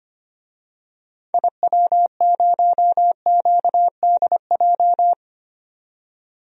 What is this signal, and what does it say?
Morse code, heard as a keyed tone, about 25 wpm, 700 Hz. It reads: IW0QDJ
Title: DXCC-100 Call Signs - 6 Characters - 25wpm